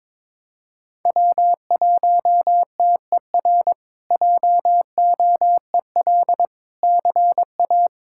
Morse code, heard as a keyed tone, about 22 wpm, 700 Hz. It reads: W1TER JOEL CA
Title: K1USN SST QSO Practice - 22wpm